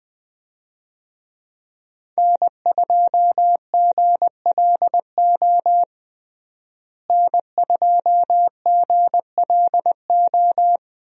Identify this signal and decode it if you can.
Morse code, heard as a keyed tone, about 21 wpm, 700 Hz. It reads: N2GLO N2GLO
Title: USA Call Signs - 5 Characters - Repeated 2x - 20wpm